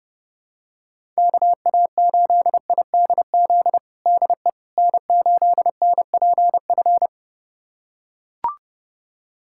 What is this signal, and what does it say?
Morse code, heard as a keyed tone, about 30 wpm, 700 Hz. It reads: KA8IDZ DE N8NPF
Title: USA Callsign DE Callsign 30wpm